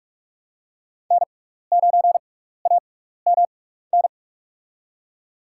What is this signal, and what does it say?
Morse code, heard as a keyed tone, about 46 wpm, spaced as for 15 wpm, 690 Hz. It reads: N9AMN
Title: USA Call Signs - 5 Characters - 6x Character Spacing - 45wpm